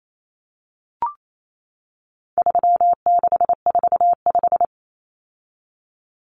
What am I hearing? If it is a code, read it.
Morse code, heard as a keyed tone, about 28 wpm, 700 Hz. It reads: 3645
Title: Common Cut Numbers - 4-Digit 28wpm